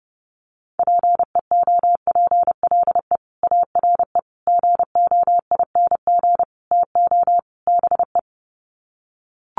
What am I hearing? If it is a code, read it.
Morse code, heard as a keyed tone, about 30 wpm, 700 Hz. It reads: PEOPLE ARE GOING TO BE
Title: Sets of 5 Words Encore 30wpm